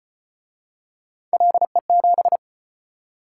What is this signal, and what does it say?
Morse code, heard as a keyed tone, about 33 wpm, 700 Hz. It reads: LE7